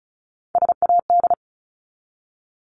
Morse code, heard as a keyed tone, about 34 wpm, 700 Hz. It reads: SAD
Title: Top 1501-1750 Words 35wpm